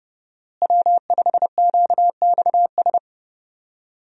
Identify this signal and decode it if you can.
Morse code, heard as a keyed tone, about 32 wpm, 700 Hz. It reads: W5QXS
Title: USA Call Signs - 5 Characters - Repeated 3x - 30wpm